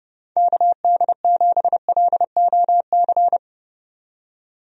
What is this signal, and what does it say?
Morse code, heard as a keyed tone, about 30 wpm, 700 Hz. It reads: KD7LOC
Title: USA Call Signs - 6 Characters - 30wpm